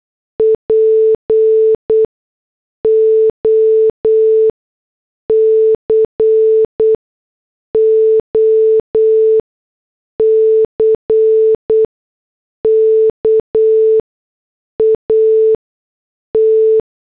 Morse code, heard as a keyed tone, about 8 wpm, spaced as for 6 wpm, 440 Hz. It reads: POCOCK AT